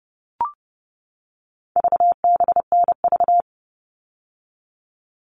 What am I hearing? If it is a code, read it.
Morse code, heard as a keyed tone, about 30 wpm, 700 Hz. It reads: VBNV
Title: All Cut Numbers - 4-Digit 30wpm